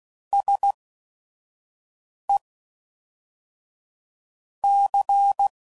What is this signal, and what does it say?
Morse code, heard as a keyed tone, about 16 wpm, 800 Hz. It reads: S E C